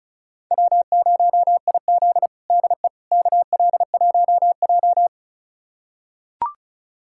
Morse code, heard as a keyed tone, about 36 wpm, 700 Hz. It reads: W0IZ DE KL1J
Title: USA Callsign DE Callsign - 4 Characters - 35wpm